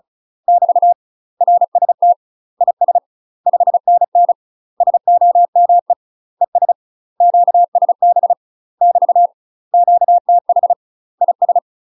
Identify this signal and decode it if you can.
Morse code, heard as a keyed tone, about 34 wpm, 700 Hz. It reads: <BT> RST IS 5NN SOME ES QSB <BT> QTH IS
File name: UK MASTER QSOs TRAINING / 35WPM / UK MASTER QSOs 0034